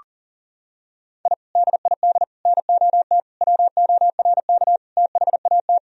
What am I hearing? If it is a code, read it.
Morse code, heard as a keyed tone, about 40 wpm, 700 Hz. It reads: I DID NOT WORK THAT
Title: Sentences from Top 1000 Words - Review Part 2 - 40wpm